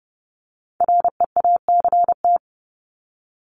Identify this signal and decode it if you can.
Morse code, heard as a keyed tone, about 30 wpm, 700 Hz. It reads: REACT
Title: Top 1251-1500 Words 30wpm